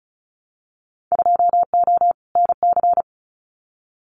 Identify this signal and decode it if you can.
Morse code, heard as a keyed tone, about 34 wpm, 700 Hz. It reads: 2O NC
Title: Winter Field Day Exchanges 35wpm